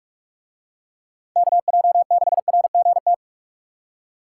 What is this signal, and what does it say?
Morse code, heard as a keyed tone, about 43 wpm, 700 Hz. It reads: KJ6RGT